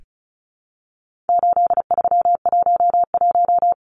Morse code, heard as a keyed tone, about 34 wpm, 700 Hz. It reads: 8311
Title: Numbers - 4-Digit 35wpm